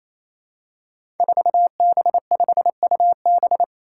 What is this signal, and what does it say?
Morse code, heard as a keyed tone, about 28 wpm, 700 Hz. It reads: 4B5UB
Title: Five Challenging Letter-Number Combinations 28wpm